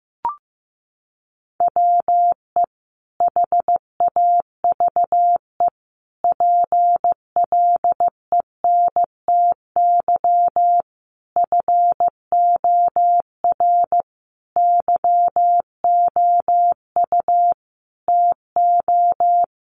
Morse code, heard as a keyed tone, about 15 wpm, 700 Hz. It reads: WE HAVE PLENTY FOR YOU TO
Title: Sentences from Top 1300 Words 15wpm